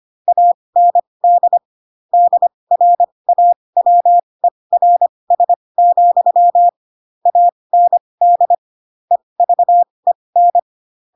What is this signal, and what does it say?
Morse code, heard as a keyed tone, about 25 wpm, spaced as for 20 wpm, 700 Hz. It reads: AND DRAWERS, AND EVEN